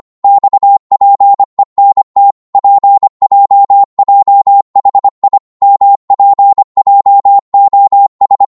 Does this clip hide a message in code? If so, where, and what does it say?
Yes: Morse code from 0.2 to 8.4 s, heard as a keyed tone, about 25 wpm, 800 Hz. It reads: XPENT PJJHI MPJOS